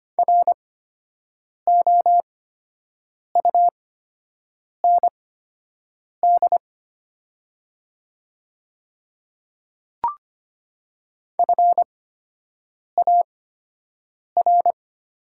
Morse code, heard as a keyed tone, about 25 wpm, spaced as for 7 wpm, 700 Hz. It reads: ROUND FAR
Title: 8x Character Spacing - Top 500 Words - 25wpm